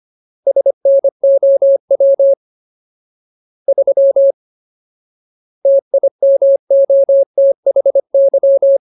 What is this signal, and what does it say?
Morse code, heard as a keyed tone, about 25 wpm, 550 Hz. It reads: SNOW 3 TIMOTHY
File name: Ultimate Word List / 25WPM Ultimate